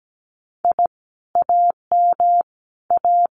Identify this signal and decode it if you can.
Morse code, heard as a keyed tone, about 17 wpm, 700 Hz. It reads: I AM A
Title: Common QSO Phrases Encore 17wpm